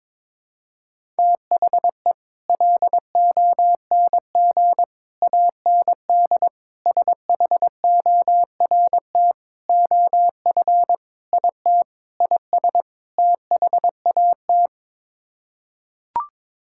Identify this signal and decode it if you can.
Morse code, heard as a keyed tone, about 22 wpm, 700 Hz. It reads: THE LONG AND SHORT OF IT IS THAT
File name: practice / 22wpm / Sets of 8 Words - 22wpm